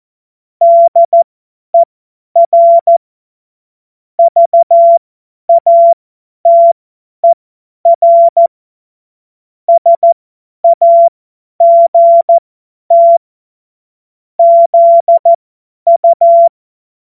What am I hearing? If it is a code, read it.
Morse code, heard as a keyed tone, about 14 wpm, spaced as for 10 wpm, 680 Hz. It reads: DER VATER SAGT ZU
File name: Morsekurs / Witze / Witz 16. Februar 2021, 50 F70